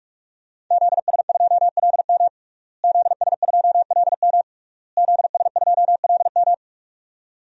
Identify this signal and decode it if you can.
Morse code, heard as a keyed tone, about 41 wpm, 700 Hz. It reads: ZS2LM ZS2LM ZS2LM
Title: DXCC-100 Call Signs - 5 Characters - Repeated 3x - 45wpm